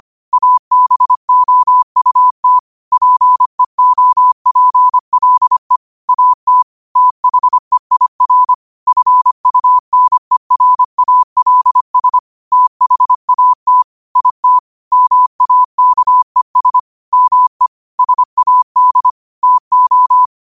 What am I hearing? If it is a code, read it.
Morse code, heard as a keyed tone, about 25 wpm, 1000 Hz. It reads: ABOUT PEOPLE AT THEIR FUNERALS THAT IT MAKES ME SAD TO
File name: 25wpm